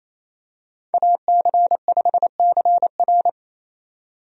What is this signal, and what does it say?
Morse code, heard as a keyed tone, about 28 wpm, 700 Hz. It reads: AC5CR
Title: USA Call Signs - 5 Characters - 28wpm